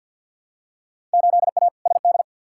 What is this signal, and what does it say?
Morse code, heard as a keyed tone, about 52 wpm, 700 Hz. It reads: 8A SD
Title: Field Day Exchanges Encore - 50wpm